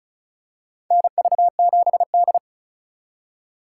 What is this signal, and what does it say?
Morse code, heard as a keyed tone, about 34 wpm, 690 Hz. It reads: NV7D
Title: USA Call Signs - 3x Character Spacing - 35wpm